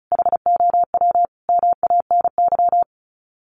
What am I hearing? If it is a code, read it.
Morse code, heard as a keyed tone, about 34 wpm, 700 Hz. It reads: HOW MANY
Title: Sets of 5 Words Encore 35wpm